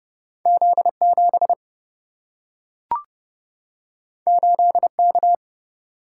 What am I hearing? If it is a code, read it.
Morse code, heard as a keyed tone, about 30 wpm, 700 Hz. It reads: Z 7 8 K
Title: Single-Letter Single-Number 30wpm